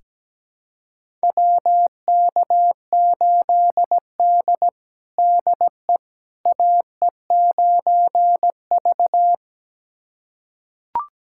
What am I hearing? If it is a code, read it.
Morse code, heard as a keyed tone, about 17 wpm, 700 Hz. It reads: WK8D DE AE9V
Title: USA Callsign DE Callsign - 4 Characters - 17wpm